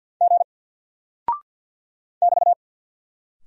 Morse code, heard as a keyed tone, about 48 wpm, 700 Hz. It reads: G <BT>
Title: Single Character (Letter:Number:Punctuation:Prosign) Alt 50wpm